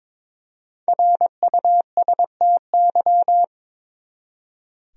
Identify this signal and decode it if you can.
Morse code, heard as a keyed tone, about 22 wpm, 700 Hz. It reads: RUSTY
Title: USA Common Names - Max 5 Letters - 22wpm